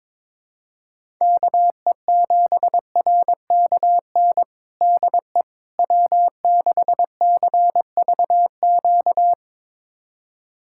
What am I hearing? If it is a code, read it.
Morse code, heard as a keyed tone, about 22 wpm, 700 Hz. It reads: KE7RKN DE W6CVQ
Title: USA Callsign DE Callsign 22wpm